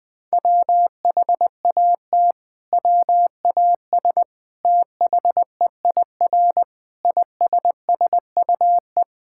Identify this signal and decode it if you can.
Morse code, heard as a keyed tone, about 21 wpm, 700 Hz. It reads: WHAT WAS THEIR ISSUE
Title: Sentences from Top 1000 Words - Review Part 6 - 20wpm